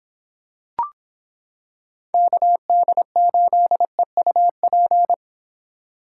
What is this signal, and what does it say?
Morse code, heard as a keyed tone, about 26 wpm, 700 Hz. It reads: KD8EUP